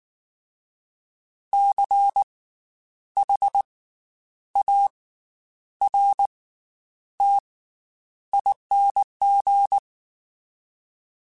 Morse code, heard as a keyed tone, about 19 wpm, spaced as for 9 wpm, 800 Hz. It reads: CHARTING
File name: Special INGs / Special INGs-mp3s / Slow ing @18